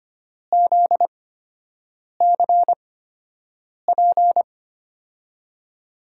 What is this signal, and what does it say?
Morse code, heard as a keyed tone, about 25 wpm, spaced as for 7 wpm, 700 Hz. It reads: ZCP
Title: USA Call Signs - 6 Characters - 8x Character Spacing - 25wpm